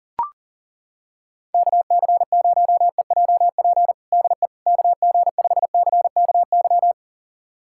Morse code, heard as a keyed tone, about 40 wpm, 700 Hz. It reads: KC0EJP DE KG5CKY
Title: USA Callsign DE Callsign - 6 Characters - 40wpm